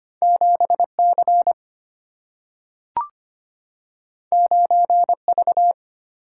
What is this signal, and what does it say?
Morse code, heard as a keyed tone, about 25 wpm, 700 Hz. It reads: 7C 9V
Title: Single-Letter Single-Number 25wpm